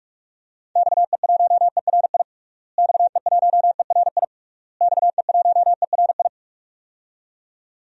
Morse code, heard as a keyed tone, about 46 wpm, 700 Hz. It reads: XE1ERI XE1ERI XE1ERI
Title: DXCC-100 Call Signs - 6 Characters - Repeated 3x - 45wpm